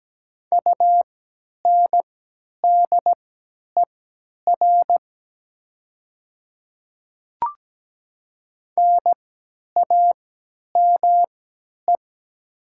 Morse code, heard as a keyed tone, about 17 wpm, spaced as for 10 wpm, 700 Hz. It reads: UNDER NAME
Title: 3x Character Spacing - Top 500 Words - 17wpm